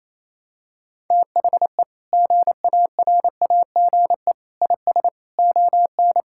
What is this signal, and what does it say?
Morse code, heard as a keyed tone, about 28 wpm, 700 Hz. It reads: THE GARAGE IS ON
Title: Sentences from Top 1600 Words 28wpm